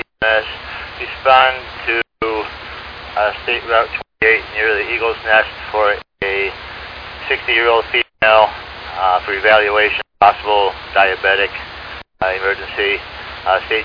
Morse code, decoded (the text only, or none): none